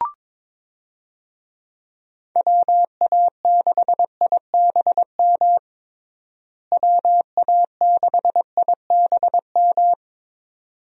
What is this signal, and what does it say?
Morse code, heard as a keyed tone, about 22 wpm, 700 Hz. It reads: WA6IBM WA6IBM
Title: USA Call Signs - 6 Characters - Repeated 2x - 22wpm